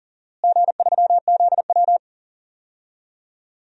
Morse code, heard as a keyed tone, about 37 wpm, 700 Hz. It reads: G3ZW
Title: DXCC-100 Call Signs - 4 Characters - Repeated 3x - 40wpm